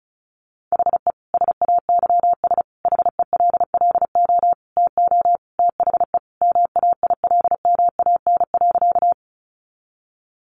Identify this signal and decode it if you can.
Morse code, heard as a keyed tone, about 34 wpm, 700 Hz. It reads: HE SAYS HELLO TO THE MAILMAN.
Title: Additional Easy Phrases 35wpm